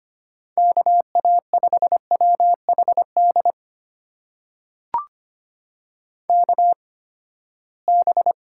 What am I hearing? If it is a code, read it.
Morse code, heard as a keyed tone, about 25 wpm, spaced as for 17 wpm, 700 Hz. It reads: KA5WHD KB
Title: USA Call Signs - 6 Characters - 8x Character Spacing - 25wpm